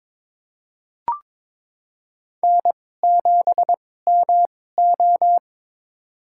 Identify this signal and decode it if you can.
Morse code, heard as a keyed tone, about 23 wpm, spaced as for 16 wpm, 700 Hz. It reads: N7MO
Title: USA Call Signs - 4 Characters - 2x Character Spacing - 22wpm